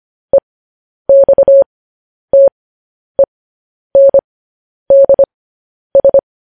Morse code, heard as a keyed tone, about 25 wpm, spaced as for 10 wpm, 550 Hz. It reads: EXTENDS